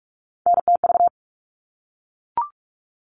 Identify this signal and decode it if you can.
Morse code, heard as a keyed tone, about 44 wpm, 700 Hz. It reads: NTV
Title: All Cut Numbers - 3-Digit 45wpm